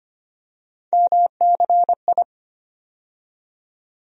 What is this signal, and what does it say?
Morse code, heard as a keyed tone, about 25 wpm, 700 Hz. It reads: MCI
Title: USA Call Sign Postfixes - 3 Letters 25wpm